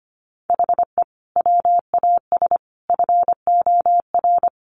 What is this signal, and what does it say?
Morse code, heard as a keyed tone, about 25 wpm, 700 Hz. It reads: HE WAS FOR
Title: Sentences from Top 1400 Words 25wpm